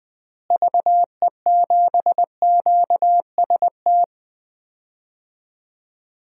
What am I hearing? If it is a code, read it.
Morse code, heard as a keyed tone, about 20 wpm, 700 Hz. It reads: VE7QST
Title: DXCC-100 Call Signs - 6 Characters - Repeated 3x - 20wpm